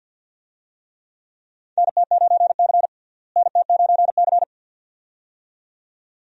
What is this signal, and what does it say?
Morse code, heard as a keyed tone, about 51 wpm, 700 Hz. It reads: NT9C NT9C
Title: USA Call Signs - 4 Characters - Repeated 2x - 50wpm